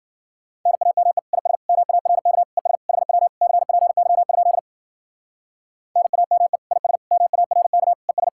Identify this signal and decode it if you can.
Morse code, heard as a keyed tone, about 52 wpm, 700 Hz. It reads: NAME IS MARK ES HW CPY? NAME IS MARK ES